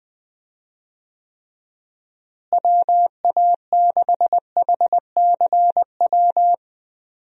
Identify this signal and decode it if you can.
Morse code, heard as a keyed tone, about 20 wpm, 700 Hz. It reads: WA6HCW
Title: USA Call Signs - 6 Characters - Repeated 2x - 20wpm